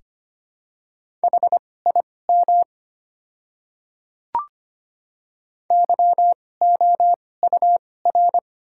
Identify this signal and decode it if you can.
Morse code, heard as a keyed tone, about 25 wpm, spaced as for 18 wpm, 690 Hz. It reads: HIM YOUR
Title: 2x Character Spacing - Top 100 Words - 25wpm